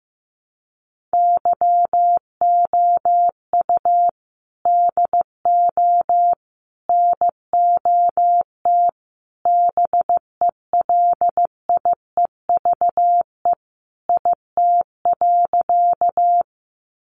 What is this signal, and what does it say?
Morse code, heard as a keyed tone, about 15 wpm, 700 Hz. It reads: YOU DO NOT BELIEVE IT.
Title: Sentences from Top 1000 Words 15wpm